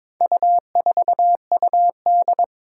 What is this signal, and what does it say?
Morse code, heard as a keyed tone, about 22 wpm, 700 Hz. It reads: U4UD
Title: Four Challenging Letter-Number Combinations 22wpm